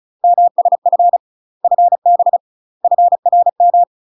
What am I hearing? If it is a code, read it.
Morse code, heard as a keyed tone, about 36 wpm, 700 Hz. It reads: MSF FB FRM
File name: UK MASTER QSOs TRAINING / 35WPM / UK MASTER QSOs 0010